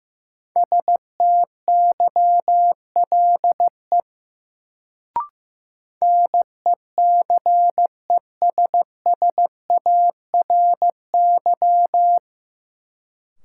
Morse code, heard as a keyed tone, about 15 wpm, 700 Hz. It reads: STYLE NECESSARY
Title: Sentences from Top 800 Words - New Word Drill - 15wpm